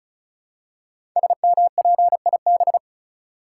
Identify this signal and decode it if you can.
Morse code, heard as a keyed tone, about 36 wpm, 700 Hz. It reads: SMPIB
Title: Code Groups - 5 Letters - 35wpm